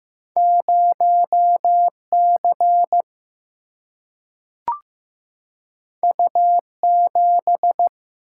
Morse code, heard as a keyed tone, about 15 wpm, 700 Hz. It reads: 0C U7
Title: Single-Letter Single-Number 15wpm